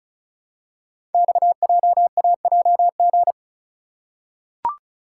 Morse code, heard as a keyed tone, about 36 wpm, 700 Hz. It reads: XJAJG
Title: Code Groups - 5 Letters - 35wpm